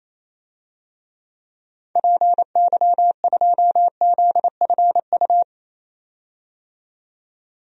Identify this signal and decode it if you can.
Morse code, heard as a keyed tone, about 28 wpm, 700 Hz. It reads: PY2ZFU
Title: DXCC-100 Call Signs - 6 Characters - 28wpm